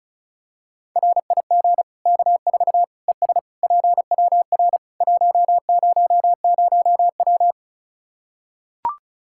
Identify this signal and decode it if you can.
Morse code, heard as a keyed tone, about 34 wpm, 700 Hz. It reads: RIG K4 ES PWR 100W
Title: Practice QSO - Exchange 2 - Info - 35wpm